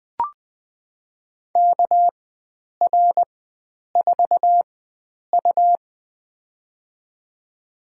Morse code, heard as a keyed tone, about 20 wpm, spaced as for 9 wpm, 700 Hz. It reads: KR4U
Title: USA Call Signs - 4 Characters - 4x Character Spacing - 20wpm